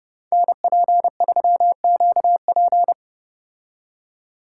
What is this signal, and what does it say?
Morse code, heard as a keyed tone, about 31 wpm, 700 Hz. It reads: NP3QP